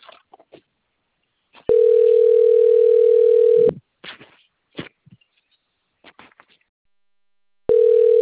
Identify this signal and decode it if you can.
Ringback tone (North America), 440 + 480 Hz.
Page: Itatiba